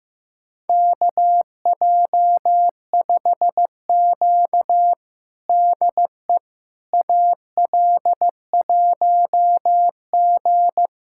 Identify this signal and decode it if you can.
Morse code, heard as a keyed tone, about 15 wpm, 700 Hz. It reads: KJ5Q DE AL1G
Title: USA Callsign DE Callsign - 4 Characters - 15wpm